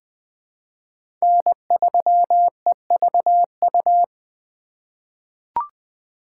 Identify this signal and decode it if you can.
Morse code, heard as a keyed tone, about 20 wpm, 700 Hz. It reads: N3EVU
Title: USA Call Signs - 5 Characters - 3x Character Spacing - 20wpm